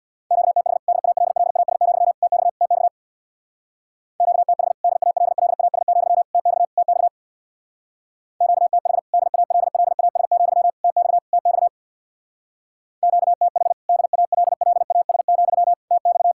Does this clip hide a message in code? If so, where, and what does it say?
Yes: Morse code from 0.3 to 16.3 s, heard as a keyed tone, about 67 wpm, 700 Hz. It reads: QTH DALLAS, TX TX QTH DALLAS, TX TX QTH DALLAS, TX TX QTH DALLAS, TX